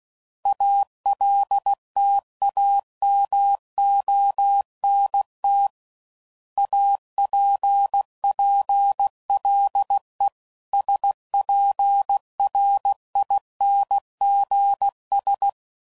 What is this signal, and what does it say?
Morse code, heard as a keyed tone, about 16 wpm, 800 Hz. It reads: ALTAMONT APPLE SPRINGS
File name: CW Code Groups @15WPM / fromSM's / Cities